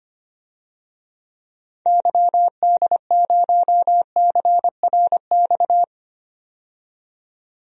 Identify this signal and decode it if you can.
Morse code, heard as a keyed tone, about 25 wpm, 700 Hz. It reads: YD0CRX
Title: DXCC-100 Call Signs - 6 Characters - 25wpm